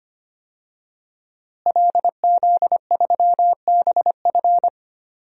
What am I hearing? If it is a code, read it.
Morse code, heard as a keyed tone, about 25 wpm, 700 Hz. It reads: LZ3BF
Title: DXCC-100 Call Signs - 5 Characters - Repeated 3x - 25wpm